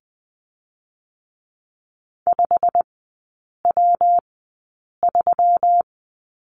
Morse code, heard as a keyed tone, about 20 wpm, spaced as for 8 wpm, 700 Hz. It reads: 5W3
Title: 138 Lesson 23e - Letter f - Three Characters 20wpm